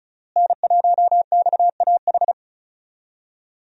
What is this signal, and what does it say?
Morse code, heard as a keyed tone, about 34 wpm, 700 Hz. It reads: N1XAH
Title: USA Call Signs - 5 Characters - Repeated 2x - 35wpm